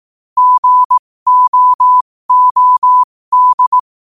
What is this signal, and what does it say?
Morse code, heard as a keyed tone, about 18 wpm, spaced as for 15 wpm, 1000 Hz. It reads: GOOD
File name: qotdfiles / 15wpm